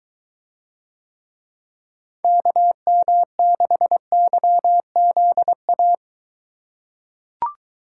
Morse code, heard as a keyed tone, about 22 wpm, 700 Hz. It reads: KM6YZA